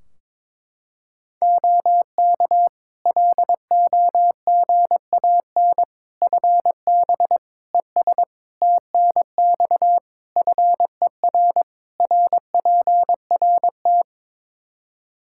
Morse code, heard as a keyed tone, about 22 wpm, 700 Hz. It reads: OK LOGAN FB ES TNX FER RPRT
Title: Practice QSO - Exchange 2 - Info - 22wpm